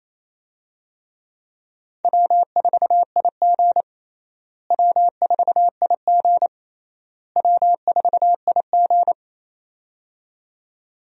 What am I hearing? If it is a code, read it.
Morse code, heard as a keyed tone, about 28 wpm, 700 Hz. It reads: W4IG W4IG W4IG
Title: USA Call Signs - 4 Characters - Repeated 3x - 28wpm